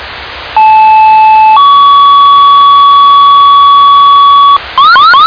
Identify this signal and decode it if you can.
Two-tone page, heard as a one-second tone, then a three-second tone, about 870 then 1130 Hz, with noise. It is heard from 0.6 to 4.6 s.